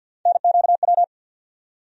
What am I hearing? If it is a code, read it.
Morse code, heard as a keyed tone, about 53 wpm, 700 Hz. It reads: NQW
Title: USA Call Sign Postfixes - 3 Letters 50wpm